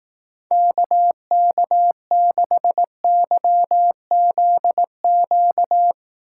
Morse code, heard as a keyed tone, about 18 wpm, 700 Hz. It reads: KK6YZQ
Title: Speed Racing - USA 2x3 Call Signs 15wpm